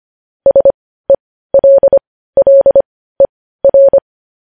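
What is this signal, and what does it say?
Morse code, heard as a keyed tone, about 25 wpm, spaced as for 15 wpm, 550 Hz. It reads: SELLER
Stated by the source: Morse code sending of news headlines at 15 wpm.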